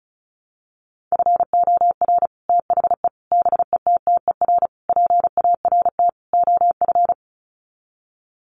Morse code, heard as a keyed tone, about 34 wpm, 700 Hz. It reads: FOR THE BETTER PART OF
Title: Sets of 5 Words Encore2 35wpm